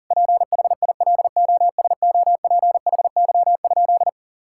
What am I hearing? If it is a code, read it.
Morse code, heard as a keyed tone, about 40 wpm, 700 Hz. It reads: PHILOSOPHY?
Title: Sentences from Top 1000 Words - Review Part 6 - 40wpm